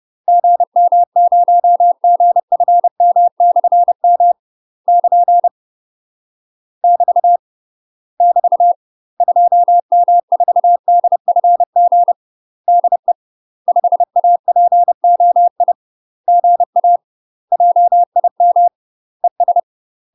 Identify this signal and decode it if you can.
Morse code, heard as a keyed tone, about 30 wpm, 700 Hz. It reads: GM0GFM/M <KN> <BT> <BT> 2M4DFG DE 5APOI GA JIM ES
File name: UK MASTER QSOs TRAINING / 30WPM / UK MASTER QSOs 0007